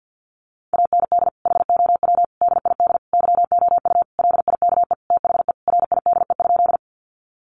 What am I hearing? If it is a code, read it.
Morse code, heard as a keyed tone, about 50 wpm, 700 Hz. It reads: AND HOW DID YOU LIKE THE RIDE?
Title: Sentences from Top 1200 Words 50wpm